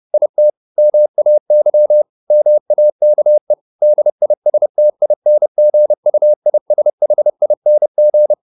Morse code, heard as a keyed tone, about 30 wpm, 600 Hz. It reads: IT MAY MAKE DISTINGUISHING